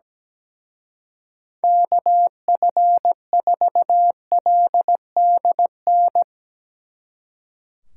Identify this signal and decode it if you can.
Morse code, heard as a keyed tone, about 17 wpm, 700 Hz. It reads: KF4LDN
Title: Speed Racing - USA 2x3 Call Signs 17wpm